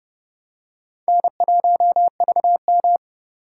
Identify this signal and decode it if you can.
Morse code, heard as a keyed tone, about 30 wpm, 690 Hz. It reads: N1VM
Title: USA Call Signs - 4 Characters - 6x Character Spacing - 30wpm